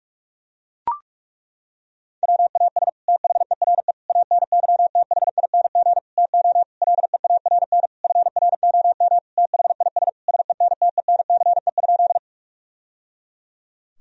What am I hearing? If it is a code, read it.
Morse code, heard as a keyed tone, about 41 wpm, 700 Hz. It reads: WAS THERE ANYTHING TO LEARN FROM THIS SENTENCE?
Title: Sentences from Top 1000 Words - Review Part 1 - 45wpm